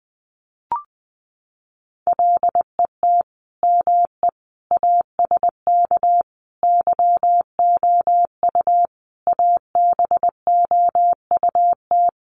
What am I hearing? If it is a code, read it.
Morse code, heard as a keyed tone, about 20 wpm, 700 Hz. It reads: LET ME ASK YOU ABOUT
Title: Sets of 5 Words 20wpm